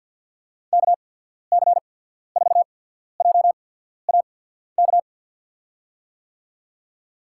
Morse code, heard as a keyed tone, about 51 wpm, spaced as for 14 wpm, 700 Hz. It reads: KC4JAK
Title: USA Call Signs - 6 Characters - 8x Character Spacing - 50wpm